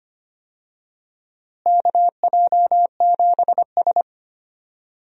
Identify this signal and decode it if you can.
Morse code, heard as a keyed tone, about 25 wpm, 700 Hz. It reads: KJ7S
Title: USA Call Signs - 4 Characters - Repeated 2x - 25wpm